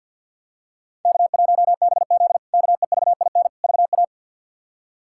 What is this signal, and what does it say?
Morse code, heard as a keyed tone, about 47 wpm, 700 Hz. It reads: K1BZ KEVIN VA
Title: K1USN SST QSO Practice - 50wpm